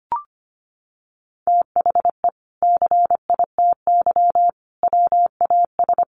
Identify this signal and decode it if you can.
Morse code, heard as a keyed tone, about 25 wpm, 700 Hz. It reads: THE CITY WAS
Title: Sentences from Top 500 Words 25wpm